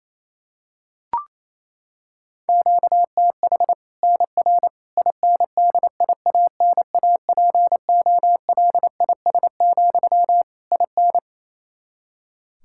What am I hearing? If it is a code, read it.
Morse code, heard as a keyed tone, about 28 wpm, 700 Hz. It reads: QTH NR INDIANAPOLIS, IN
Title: Practice QSO - Exchange 1 - Beginning - 28wpm